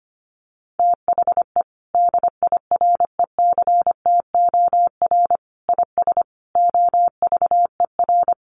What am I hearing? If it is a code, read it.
Morse code, heard as a keyed tone, about 25 wpm, 700 Hz. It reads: THE DIRECTOR IS OVER
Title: Sentences from Top 1300 Words 25wpm